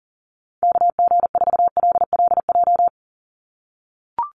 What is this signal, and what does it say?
Morse code, heard as a keyed tone, about 40 wpm, 700 Hz. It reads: KG4LLJ
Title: USA Call Signs 40wpm